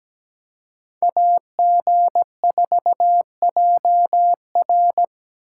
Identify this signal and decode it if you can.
Morse code, heard as a keyed tone, about 20 wpm, 700 Hz. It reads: AG4JR
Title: Speed Racing - USA 2x2 Call Signs 17wpm